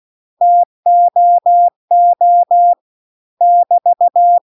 Morse code, heard as a keyed tone, about 16 wpm, 700 Hz. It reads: TOO <BT>
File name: CW-The-Big-Twoheaded-River-2-16WPM